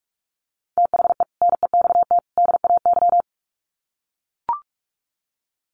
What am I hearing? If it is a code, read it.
Morse code, heard as a keyed tone, about 44 wpm, 700 Hz. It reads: THE NEXT DAY
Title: Sets of 3 Words 45wpm